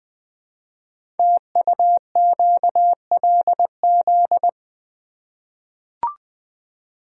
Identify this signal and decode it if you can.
Morse code, heard as a keyed tone, about 20 wpm, 700 Hz. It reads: TUQLZ
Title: Code Groups - 5 Letters - 20wpm